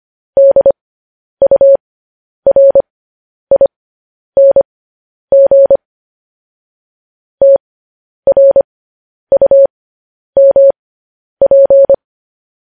Morse code, heard as a keyed tone, about 25 wpm, spaced as for 10 wpm, 550 Hz. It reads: DURING TRUMP